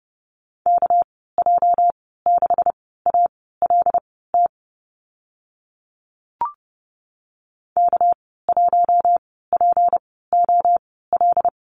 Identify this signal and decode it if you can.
Morse code, heard as a keyed tone, about 30 wpm, spaced as for 17 wpm, 700 Hz. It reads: KJ6ALT K1POL
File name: Lesson 37g - Letter x - Call Signs 30wpm